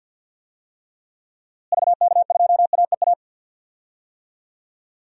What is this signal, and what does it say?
Morse code, heard as a keyed tone, about 49 wpm, 700 Hz. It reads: VK2AEA
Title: DXCC-100 Call Signs - 6 Characters - Repeated 2x - 50wpm